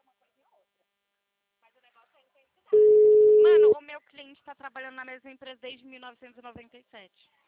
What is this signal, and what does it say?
Ringback tone (Europe, 425 Hz), with speech.